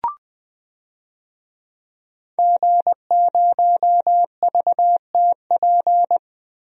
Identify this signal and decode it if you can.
Morse code, heard as a keyed tone, about 19 wpm, 700 Hz. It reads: G0VTP
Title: DXCC-100 Call Signs - 5 Characters - Repeated 3x - 20wpm